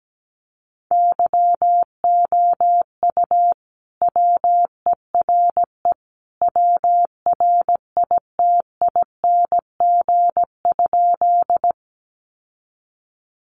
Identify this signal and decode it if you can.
Morse code, heard as a keyed tone, about 17 wpm, 700 Hz. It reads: YOU WERE WRITING?
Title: Sentences from Top 800 Words 17wpm